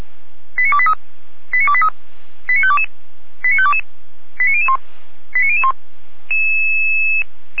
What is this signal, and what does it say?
Five-tone selective call (ZVEI), heard as quick runs of five tones, tones of about 70 ms. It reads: 89283 89283 89522 89522 89001 89001